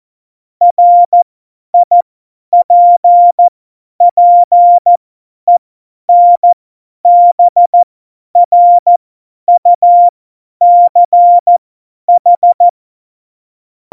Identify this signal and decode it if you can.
Morse code, heard as a keyed tone, about 14 wpm, spaced as for 10 wpm, 700 Hz. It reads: RIPPENBRUCH
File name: Morsekurs / Witze / Witz 29. September 2020, 50 F70